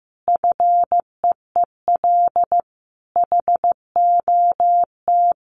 Morse code, heard as a keyed tone, about 15 wpm, 700 Hz. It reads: FEEL HOT
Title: Easy Phrases 15wpm